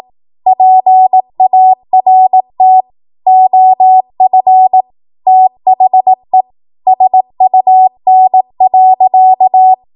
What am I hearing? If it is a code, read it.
Morse code, heard as a keyed tone, about 18 wpm, 750 Hz. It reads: PART OF THE SUN.
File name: Morse / 18WPM